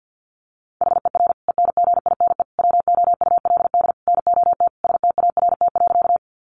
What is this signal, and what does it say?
Morse code, heard as a keyed tone, about 50 wpm, 700 Hz. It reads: HER ENGINE WOULD NOT START.
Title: Sentences from Top 1500 Words 50wpm